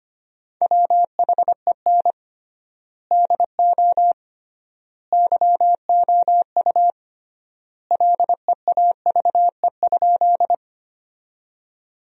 Morse code, heard as a keyed tone, about 25 wpm, 700 Hz. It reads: WHEN DO YOU LEAVE?
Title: 3x Word-Spacing - Sentences from Top 400 Words - 25wpm